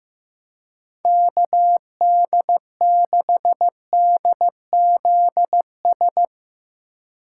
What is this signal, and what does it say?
Morse code, heard as a keyed tone, about 15 wpm, 700 Hz. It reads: KD6DZS
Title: USA Call Signs - 3x Character Spacing - 15wpm